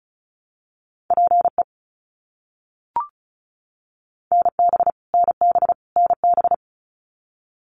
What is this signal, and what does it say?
Morse code, heard as a keyed tone, about 36 wpm, 700 Hz. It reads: PE NB NB NB
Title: Canadian Provinces and Territories - Repeated 3x - 35wpm